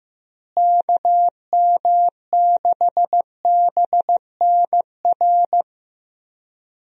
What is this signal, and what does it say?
Morse code, heard as a keyed tone, about 15 wpm, 700 Hz. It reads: KM6BNR
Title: USA Call Signs - 6 Characters - Repeated 3x - 15wpm